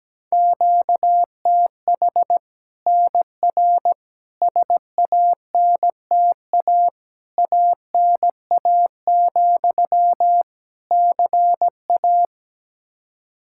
Morse code, heard as a keyed tone, about 17 wpm, 700 Hz. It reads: QTH NR SANTA ANA, CA
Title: Practice QSO - Exchange 1 - Beginning - 17wpm